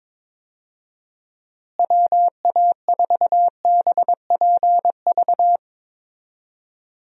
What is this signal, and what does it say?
Morse code, heard as a keyed tone, about 23 wpm, 700 Hz. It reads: WA4BPV
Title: USA Call Signs - 6 Characters - Repeated 3x - 22wpm